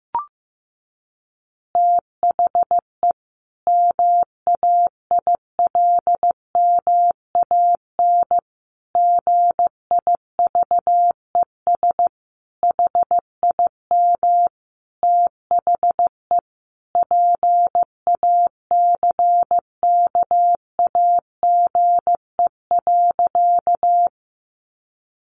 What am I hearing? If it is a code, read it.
Morse code, heard as a keyed tone, about 15 wpm, 700 Hz. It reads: THE MAILMAN GIVES HIM THE PACKAGE.
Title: Easy Phrases 15wpm